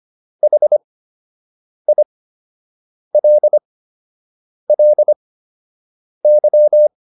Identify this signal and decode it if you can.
Morse code, heard as a keyed tone, about 25 wpm, spaced as for 7 wpm, 600 Hz. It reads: HILLY